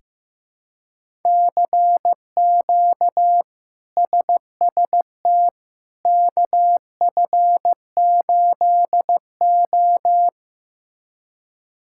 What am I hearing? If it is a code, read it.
Morse code, heard as a keyed tone, about 15 wpm, 700 Hz. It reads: CQ SST KF8O
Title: K1USN SST QSO Practice - 15wpm